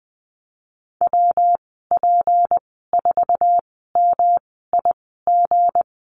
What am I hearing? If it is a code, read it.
Morse code, heard as a keyed tone, about 20 wpm, spaced as for 15 wpm, 700 Hz. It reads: WP4MIG
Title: 179 Lesson 29g - Slash - Call Signs 20wpm